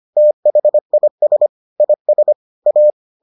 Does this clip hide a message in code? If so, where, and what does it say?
Yes: Morse code from 0.2 to 2.9 s, heard as a keyed tone, about 25 wpm, 600 Hz. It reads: THIS IS A